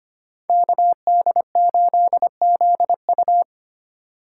Morse code, heard as a keyed tone, about 33 wpm, 700 Hz. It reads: KD8ZU